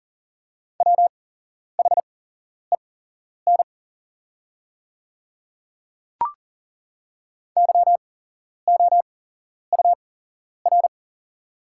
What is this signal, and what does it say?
Morse code, heard as a keyed tone, about 40 wpm, spaced as for 11 wpm, 700 Hz. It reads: WHEN YOUR
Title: 8x Character Spacing - Top 100 Words - 40wpm